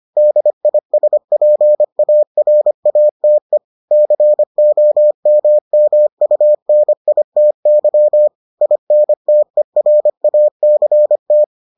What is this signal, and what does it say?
Morse code, heard as a keyed tone, about 25 wpm, 600 Hz. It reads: DISPARATE COMMUNITY INTERACT